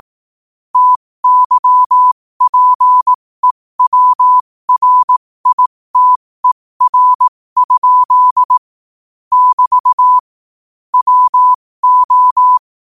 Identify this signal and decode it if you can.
Morse code, heard as a keyed tone, about 18 wpm, spaced as for 15 wpm, 1000 Hz. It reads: TYPEWRITER? <BT> WO
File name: qotdfiles / 15wpm